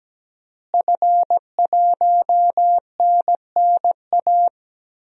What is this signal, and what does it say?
Morse code, heard as a keyed tone, about 17 wpm, 700 Hz. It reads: F1NNA
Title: DXCC-100 Call Signs - 5 Characters - Repeated 3x - 17wpm